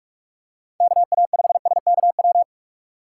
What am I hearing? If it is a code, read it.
Morse code, heard as a keyed tone, about 46 wpm, 700 Hz. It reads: KA5SKW
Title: USA Call Signs - 6 Characters - 8x Character Spacing - 45wpm